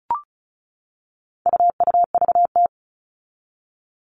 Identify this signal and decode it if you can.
Morse code, heard as a keyed tone, about 34 wpm, 700 Hz. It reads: UUVT
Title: All Cut Numbers - 4-Digit 35wpm